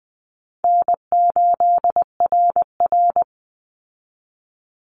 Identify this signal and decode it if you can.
Morse code, heard as a keyed tone, about 20 wpm, 700 Hz. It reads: N8RR
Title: Easy Call Signs Repeated 2x - 20wpm